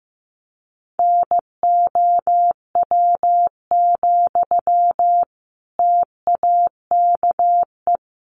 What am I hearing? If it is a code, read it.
Morse code, heard as a keyed tone, about 15 wpm, 700 Hz. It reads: NOW, TAKE
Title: Sentences from Top 1500 Words 15wpm